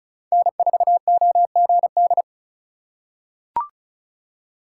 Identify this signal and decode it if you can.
Morse code, heard as a keyed tone, about 36 wpm, 700 Hz. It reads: N4OGD
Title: USA Call Signs - 5 Characters - 4x Character Spacing - 35wpm